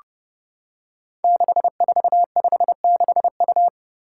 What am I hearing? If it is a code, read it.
Morse code, heard as a keyed tone, about 28 wpm, 700 Hz. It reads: 6456U
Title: Five Challenging Letter-Number Combinations 30wpm